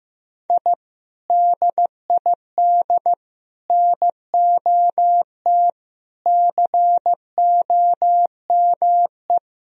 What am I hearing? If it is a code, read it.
Morse code, heard as a keyed tone, about 15 wpm, 700 Hz. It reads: I DID NOT COME
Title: Sentences from Top 1000 Words - Review Part 1 - 15wpm